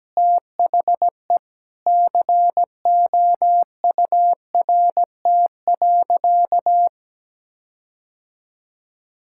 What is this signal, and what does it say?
Morse code, heard as a keyed tone, about 17 wpm, 700 Hz. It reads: THE COURT.
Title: Sentences from Top 1600 Words 17wpm